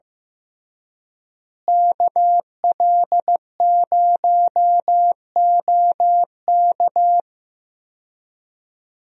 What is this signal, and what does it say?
Morse code, heard as a keyed tone, about 15 wpm, 700 Hz. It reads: KL0OK
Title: USA Call Signs - 5 Characters - Repeated 2x - 15wpm